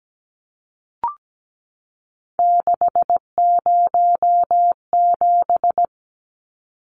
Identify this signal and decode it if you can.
Morse code, heard as a keyed tone, about 17 wpm, 700 Hz. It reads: 607
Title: Numbers - 3-Digit 17wpm